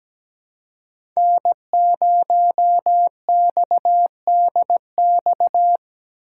Morse code, heard as a keyed tone, about 20 wpm, 700 Hz. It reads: N0XDX
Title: Speed Racing - USA Call Signs 17wpm